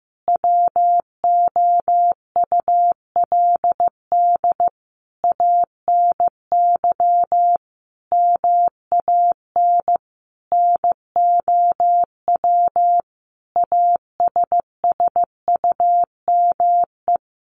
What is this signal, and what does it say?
Morse code, heard as a keyed tone, about 15 wpm, 700 Hz. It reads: WOULD ANY MAN NOW ASSUME
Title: Sentences from Top 1500 Words 15wpm